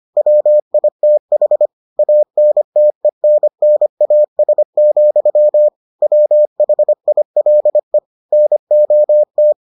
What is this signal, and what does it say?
Morse code, heard as a keyed tone, about 25 wpm, 600 Hz. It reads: WITH ANTENNAS, WHILE NOT